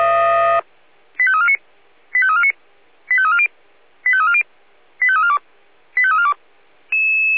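Five-tone selective call (ZVEI), heard as quick runs of five tones, tones of about 70 ms. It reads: 85349 85349 85340 85340 85342 85342
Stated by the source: ZVEI five-tone selective call received on SDR.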